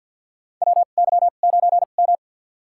Morse code, heard as a keyed tone, about 49 wpm, spaced as for 36 wpm, 700 Hz. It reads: WY9M
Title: USA Call Signs - 4 Characters - 2x Character Spacing - 50wpm